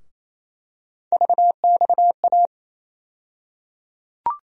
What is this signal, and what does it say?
Morse code, heard as a keyed tone, about 28 wpm, 700 Hz. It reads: VXA
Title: USA Call Sign Postfixes - 3 Letters 28wpm